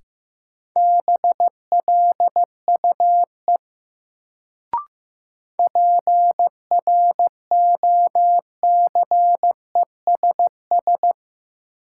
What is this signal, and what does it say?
Morse code, heard as a keyed tone, about 15 wpm, 700 Hz. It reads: BLUE PROCESS
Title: Sentences from Top 1000 Words - Review Part 3 - New Word Drill - 15wpm